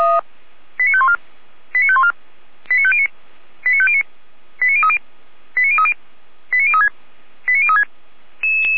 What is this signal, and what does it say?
Five-tone selective call (ZVEI), heard as quick runs of five tones, tones of about 70 ms. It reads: 89514 89514 89509 89509 89030 89030 89036 89036